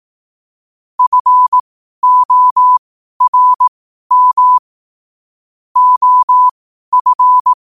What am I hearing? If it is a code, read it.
Morse code, heard as a keyed tone, about 18 wpm, spaced as for 12 wpm, 1000 Hz. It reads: FORM OF
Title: 12 WPM morse code quotes for Fri, 01 Mar 2024